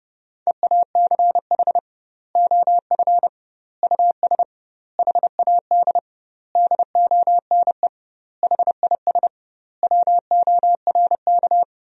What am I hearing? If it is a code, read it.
Morse code, heard as a keyed tone, about 30 wpm, 700 Hz. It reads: EACH OF US HAD DONE HIS WORK
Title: 2x Word-Spacing - Sentences from Top 500 Words - 30wpm